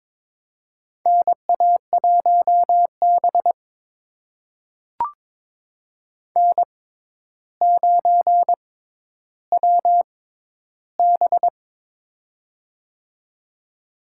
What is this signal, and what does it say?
Morse code, heard as a keyed tone, about 22 wpm, spaced as for 11 wpm, 700 Hz. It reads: NA1B N9WB
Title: USA Call Signs - 4 Characters - 6x Character Spacing - 22wpm